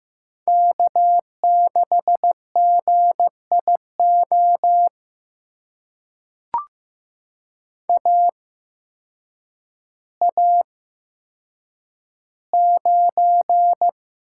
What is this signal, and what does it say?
Morse code, heard as a keyed tone, about 15 wpm, spaced as for 8 wpm, 700 Hz. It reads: K6GIO AA9
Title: USA Call Signs - 5 Characters - 8x Character Spacing - 15wpm